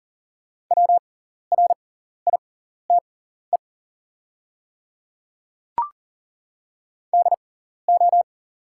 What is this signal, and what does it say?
Morse code, heard as a keyed tone, about 40 wpm, spaced as for 14 wpm, 690 Hz. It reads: WRITE DO
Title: 6x Character Spacing - Top 100 Words - 40wpm